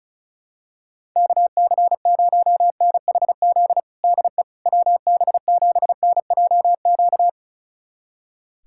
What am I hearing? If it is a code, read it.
Morse code, heard as a keyed tone, about 34 wpm, 690 Hz. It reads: KC0NHZ DE WB7NJQ
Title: USA Callsign DE Callsign - 6 Characters - 35wpm